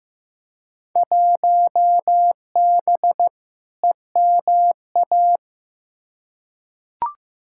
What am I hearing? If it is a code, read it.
Morse code, heard as a keyed tone, about 15 wpm, 700 Hz. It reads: 1B EMA
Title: Field Day Exchanges - 15wpm